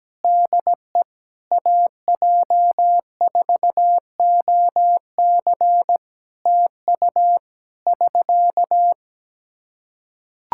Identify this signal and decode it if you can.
Morse code, heard as a keyed tone, about 17 wpm, 700 Hz. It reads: DE AJ4OC TU <SK>
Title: Practice QSO - Exchange 4 - Ending - 17wpm